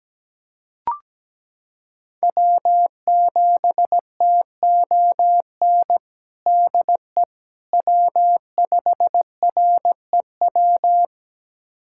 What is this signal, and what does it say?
Morse code, heard as a keyed tone, about 17 wpm, 700 Hz. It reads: W7TON DE W5REW
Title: USA Callsign DE Callsign - 5 Characters - 17wpm